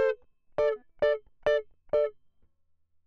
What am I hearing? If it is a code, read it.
There is no code.